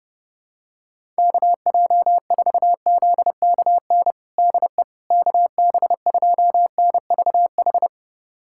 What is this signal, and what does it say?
Morse code, heard as a keyed tone, about 30 wpm, 690 Hz. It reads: KJ4ZKN DE KB2NVH
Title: USA Callsign DE Callsign - 6 Characters - 30wpm